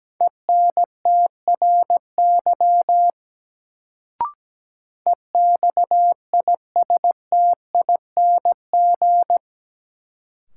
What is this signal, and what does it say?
Morse code, heard as a keyed tone, about 17 wpm, 700 Hz. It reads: ENTRY EXISTING
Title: Sentences from Top 1500 Words - New Word Drill - 17wpm